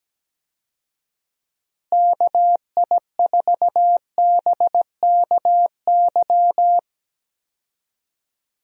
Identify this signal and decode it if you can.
Morse code, heard as a keyed tone, about 17 wpm, 700 Hz. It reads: KI4BKY
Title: USA Call Signs - 6 Characters - Repeated 2x - 17wpm